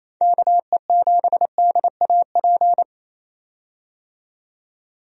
Morse code, heard as a keyed tone, about 28 wpm, 700 Hz. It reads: KE7DAP
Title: USA Call Signs - 6 Characters - 28wpm